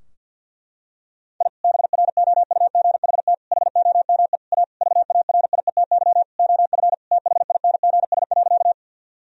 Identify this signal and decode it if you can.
Morse code, heard as a keyed tone, about 50 wpm, 700 Hz. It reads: I BROUGHT HOME A VARIETY OF THINGS.
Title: Sentences from Top 1000 Words - Review Part 4 - 50wpm